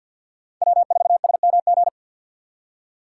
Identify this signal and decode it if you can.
Morse code, heard as a keyed tone, about 48 wpm, 700 Hz. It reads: W4SMG
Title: USA Call Signs - 5 Characters - 6x Character Spacing - 50wpm